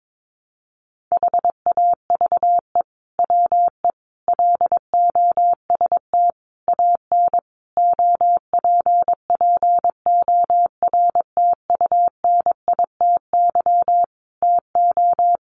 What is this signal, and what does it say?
Morse code, heard as a keyed tone, about 22 wpm, 700 Hz. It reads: HAVE WE LOST AN OPPORTUNITY TO
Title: Sentences from Top 1400 Words 22wpm